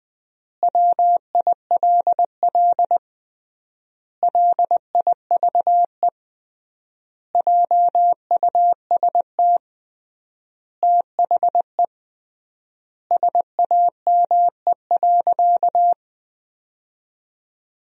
Morse code, heard as a keyed tone, about 20 wpm, 700 Hz. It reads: WILL LIVE JUST THE SAME.
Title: 3x Word-Spacing - Sentences from Top 300 Words - 20wpm